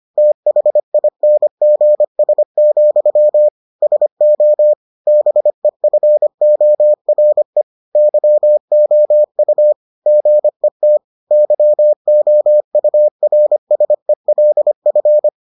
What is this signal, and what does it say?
Morse code, heard as a keyed tone, about 25 wpm, 600 Hz. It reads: THINGS, SO BEFORE YOU GET YOURSELF